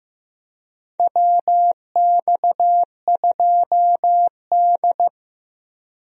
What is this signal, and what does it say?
Morse code, heard as a keyed tone, about 15 wpm, 700 Hz. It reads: WX2D
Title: USA Call Signs - 4 Characters - Repeated 2x - 15wpm